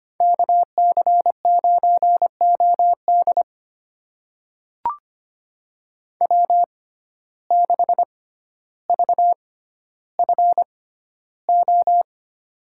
Morse code, heard as a keyed tone, about 25 wpm, spaced as for 13 wpm, 700 Hz. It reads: KC9OD W6VFO
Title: USA Call Signs - 5 Characters - 6x Character Spacing - 25wpm